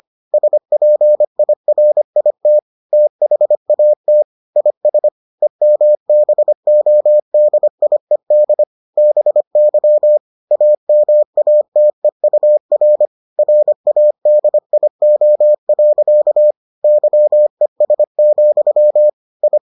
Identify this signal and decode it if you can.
Morse code, heard as a keyed tone, about 25 wpm, 600 Hz. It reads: SPIRIT THAT IS EMBODIED BY AMATEUR RADIO. YES, I